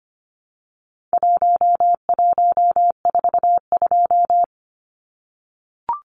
Morse code, heard as a keyed tone, about 25 wpm, 700 Hz. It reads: 1142
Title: Numbers - Contest Style - 4-Digit 25wpm